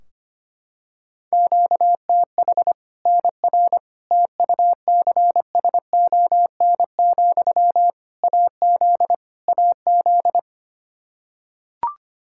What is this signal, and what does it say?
Morse code, heard as a keyed tone, about 25 wpm, 700 Hz. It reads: QTH NR TUCSON, AZ AZ
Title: Practice QSO - Exchange 1 - Beginning - 25wpm